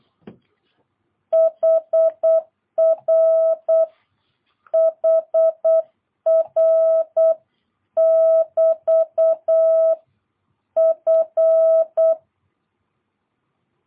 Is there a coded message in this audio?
Yes: Morse code from 1.3 to 12.1 s, heard as a keyed tone, about 8 wpm, 650 Hz. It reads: HR HR <BT> F